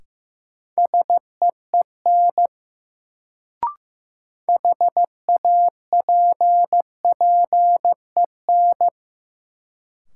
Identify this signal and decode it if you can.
Morse code, heard as a keyed tone, about 15 wpm, 700 Hz. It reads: SEEN HAPPEN
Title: Sets of 5 Words Encore - New Word Drill - 15wpm